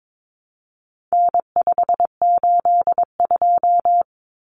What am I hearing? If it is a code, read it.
Morse code, heard as a keyed tone, about 22 wpm, 700 Hz. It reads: N582
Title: Common Cut Numbers - 4-Digit 22wpm